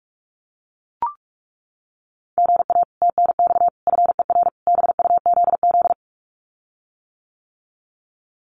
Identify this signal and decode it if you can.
Morse code, heard as a keyed tone, about 44 wpm, 700 Hz. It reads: GA TNX FER BUZZ
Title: Common QSO Phrases Encore 45wpm